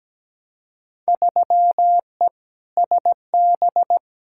Morse code, heard as a keyed tone, about 17 wpm, 700 Hz. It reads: 3E SB
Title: Field Day Exchanges Encore - 17wpm